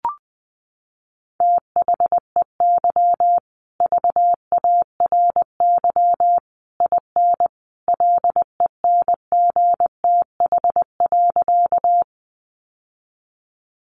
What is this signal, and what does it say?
Morse code, heard as a keyed tone, about 20 wpm, 700 Hz. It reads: THEY VARY IN LENGTH.
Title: Sentences from Top 1200 Words 20wpm